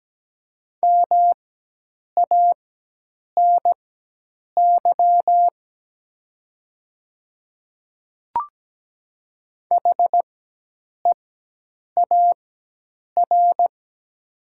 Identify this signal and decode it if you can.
Morse code, heard as a keyed tone, about 17 wpm, spaced as for 8 wpm, 700 Hz. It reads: MANY HEAR
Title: 4x Character Spacing - Top 300 Words - 17wpm